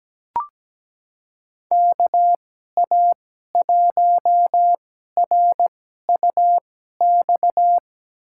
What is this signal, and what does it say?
Morse code, heard as a keyed tone, about 17 wpm, spaced as for 12 wpm, 700 Hz. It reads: KA1RUX
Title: USA Call Signs - 6 Characters - 2x Character Spacing - 17wpm